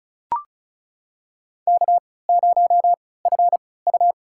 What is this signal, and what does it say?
Morse code, heard as a keyed tone, about 36 wpm, spaced as for 20 wpm, 700 Hz. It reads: K0FU
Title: USA Call Signs - 3x Character Spacing - 35wpm